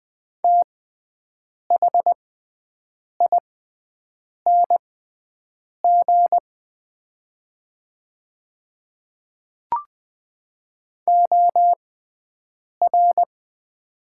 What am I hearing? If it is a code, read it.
Morse code, heard as a keyed tone, about 20 wpm, spaced as for 7 wpm, 700 Hz. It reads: THING OR
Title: 6x Character Spacing - Top 100 Words - 20wpm